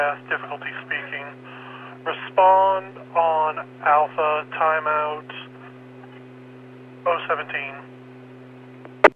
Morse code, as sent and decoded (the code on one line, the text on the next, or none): none